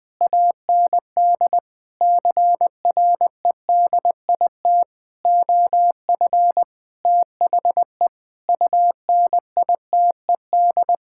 Morse code, heard as a keyed tone, about 21 wpm, 700 Hz. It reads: AND CREDIT OF THE UNITED
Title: Sets of 9 Words - 20wpm